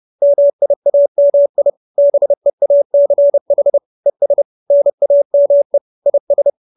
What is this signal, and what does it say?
Morse code, heard as a keyed tone, about 30 wpm, 570 Hz. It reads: MIAMI BEACH ES NAME IS
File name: QSO317 30